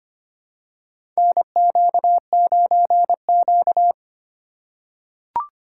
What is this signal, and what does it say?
Morse code, heard as a keyed tone, about 25 wpm, 700 Hz. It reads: NQ9Q
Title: USA Call Signs - 4 Characters - 8x Character Spacing - 25wpm